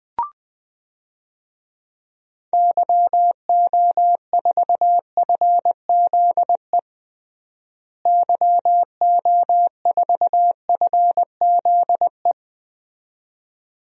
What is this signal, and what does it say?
Morse code, heard as a keyed tone, about 20 wpm, 700 Hz. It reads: YO4FZE YO4FZE
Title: DXCC-100 Call Signs - 6 Characters - Repeated 2x - 20wpm